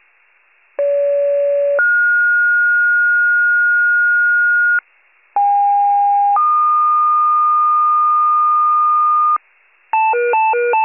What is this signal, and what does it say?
Two-tone page, heard as a one-second tone, then a three-second tone, about 570 then 1430 Hz, about 810 then 1220 Hz, with noise. It is heard from 0.8 to 9.4 s.